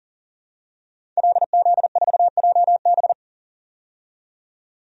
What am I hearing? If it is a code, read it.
Morse code, heard as a keyed tone, about 40 wpm, 700 Hz. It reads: LZ4JB
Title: DXCC-100 Call Signs - 5 Characters - 40wpm